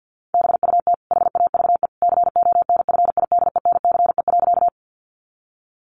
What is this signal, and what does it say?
Morse code, heard as a keyed tone, about 50 wpm, 700 Hz. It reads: BUT HAVE CONFIDENCE.
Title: Sentences from Top 1500 Words 50wpm